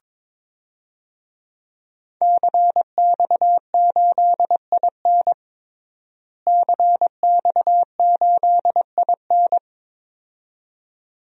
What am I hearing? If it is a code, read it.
Morse code, heard as a keyed tone, about 22 wpm, 700 Hz. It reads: CX8IN CX8IN
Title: DXCC-100 Call Signs - 5 Characters - Repeated 2x - 22wpm